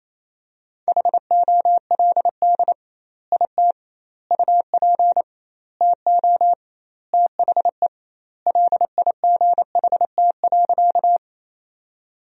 Morse code, heard as a keyed tone, about 28 wpm, 700 Hz. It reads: HOLD IT UP TO THE LIGHT.
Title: 2x Word-Spacing - Sentences from Top 400 Words - 28wpm